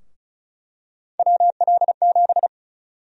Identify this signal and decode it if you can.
Morse code, heard as a keyed tone, about 36 wpm, 700 Hz. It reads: WL7
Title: USA Call Sign Prefixes - 35wpm